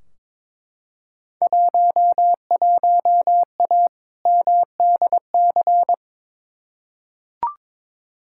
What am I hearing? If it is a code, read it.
Morse code, heard as a keyed tone, about 22 wpm, 700 Hz. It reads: 11A MDC
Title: Field Day Exchanges Encore - 22wpm